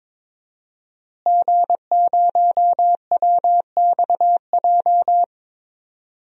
Morse code, heard as a keyed tone, about 22 wpm, 700 Hz. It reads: G0WXJ
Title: DXCC-100 Call Signs - 5 Characters - Repeated 2x - 22wpm